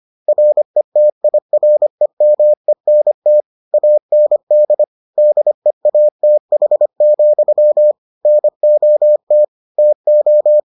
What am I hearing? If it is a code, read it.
Morse code, heard as a keyed tone, about 25 wpm, 600 Hz. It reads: RETIREMENT AND DEATH, NOT TO